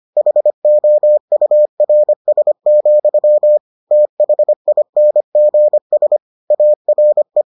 Morse code, heard as a keyed tone, about 25 wpm, 600 Hz. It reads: HOURS, THINGS ARE